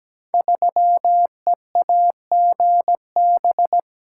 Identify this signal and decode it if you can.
Morse code, heard as a keyed tone, about 17 wpm, 700 Hz. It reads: 3EAGB
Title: Code Groups - 5 Letter:Numbers - 17wpm